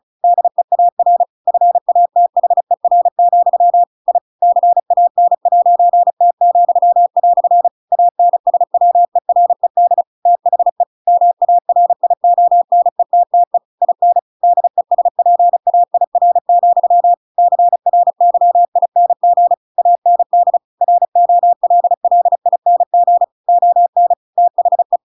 Morse code, heard as a keyed tone, about 34 wpm, 700 Hz. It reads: DEAR FATHER, I CAN'T," ANSWERED THE MARIONETTE IN DESPAIR, CRYING AND ROLLING ON THE
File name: Pinocchio / Pinocchio ch7 35 35 ik1hge